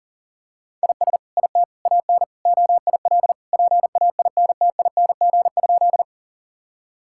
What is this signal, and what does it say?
Morse code, heard as a keyed tone, about 40 wpm, 700 Hz. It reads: IS IT AN OIL PAINTING?
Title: Sentences from Top 1000 Words - Review Part 6 - 40wpm